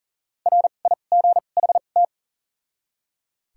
Morse code, heard as a keyed tone, about 40 wpm, spaced as for 29 wpm, 700 Hz. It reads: RIGHT
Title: 2x Character Spacing - Top 200 Words - 40wpm